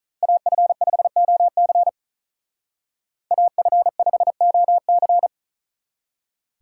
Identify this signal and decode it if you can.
Morse code, heard as a keyed tone, about 44 wpm, 700 Hz. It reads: AF5OC AF5OC
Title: Speed Racing - USA 2x2 Call Signs 35wpm